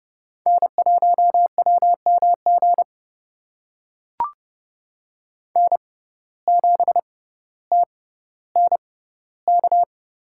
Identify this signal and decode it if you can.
Morse code, heard as a keyed tone, about 30 wpm, spaced as for 15 wpm, 700 Hz. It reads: N1WMG N7TNK
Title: USA Call Signs - 5 Characters - 6x Character Spacing - 30wpm